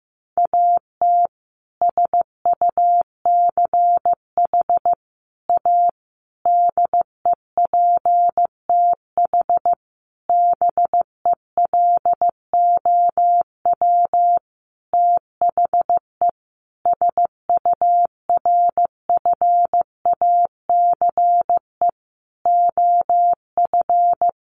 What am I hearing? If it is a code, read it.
Morse code, heard as a keyed tone, about 15 wpm, 700 Hz. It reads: AT SUCH A DEPTH BELOW THE SURFACE OF
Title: Sentences from Top 1200 Words 15wpm